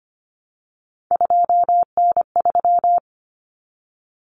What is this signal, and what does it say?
Morse code, heard as a keyed tone, about 25 wpm, 700 Hz. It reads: 2N3
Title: Common Cut Numbers - 3-Digit 25wpm